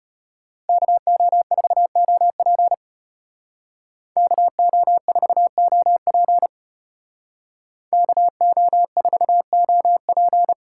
Morse code, heard as a keyed tone, about 34 wpm, 700 Hz. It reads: KO4OP KO4OP KO4OP